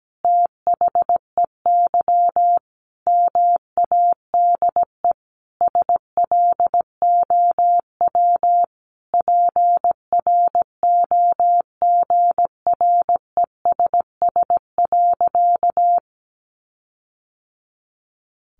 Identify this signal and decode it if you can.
Morse code, heard as a keyed tone, about 17 wpm, 700 Hz. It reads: THEY MADE SLOW PROGRESS.
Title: Sentences from Top 1400 Words 17wpm